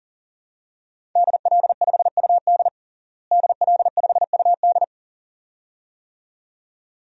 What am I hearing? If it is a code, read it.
Morse code, heard as a keyed tone, about 40 wpm, 700 Hz. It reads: DL5UD DL5UD
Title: DXCC-100 Call Signs - 5 Characters - Repeated 2x - 40wpm